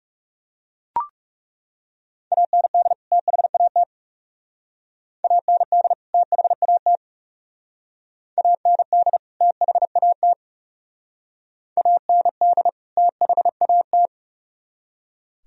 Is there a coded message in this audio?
Yes: Morse code from 2.3 to 14.1 s, heard as a keyed tone, about 40 wpm, 700 Hz. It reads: AND THAT AND THAT AND THAT AND THAT